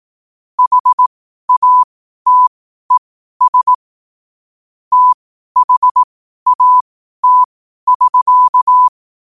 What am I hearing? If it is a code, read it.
Morse code, heard as a keyed tone, about 18 wpm, spaced as for 12 wpm, 1000 Hz. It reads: HATES THAT<SK>
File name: qotdfiles / 12wpm